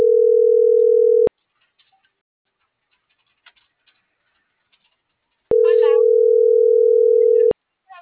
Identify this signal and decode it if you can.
Ringback tone (North America), 440 + 480 Hz.